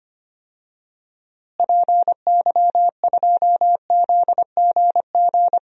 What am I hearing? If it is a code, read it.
Morse code, heard as a keyed tone, about 25 wpm, 700 Hz. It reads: PY2ZGG
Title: DXCC-100 Call Signs - 6 Characters - Repeated 2x - 25wpm